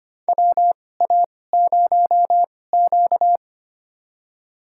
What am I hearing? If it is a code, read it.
Morse code, heard as a keyed tone, about 25 wpm, spaced as for 18 wpm, 700 Hz. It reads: WA0Q
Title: USA Call Signs - 2x Character Spacing - 25wpm